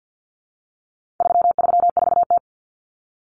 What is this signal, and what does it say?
Morse code, heard as a keyed tone, about 50 wpm, 700 Hz. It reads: 334T
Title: Common Cut Numbers - 4-Digit 50wpm